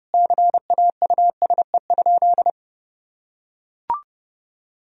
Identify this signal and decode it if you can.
Morse code, heard as a keyed tone, about 30 wpm, 700 Hz. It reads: CAUSE?
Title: Sentences from Top 1000 Words - Review Part 1 - 30wpm